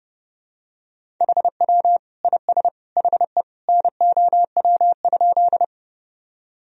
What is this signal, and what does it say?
Morse code, heard as a keyed tone, about 30 wpm, 700 Hz. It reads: HW IS HE NOW?
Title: Sentences from Top 100 Words and Abbreviations 30wpm